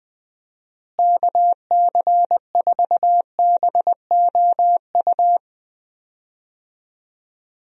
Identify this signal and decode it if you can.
Morse code, heard as a keyed tone, about 19 wpm, 700 Hz. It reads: KC4BOU
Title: USA Call Signs - 6 Characters - Repeated 2x - 20wpm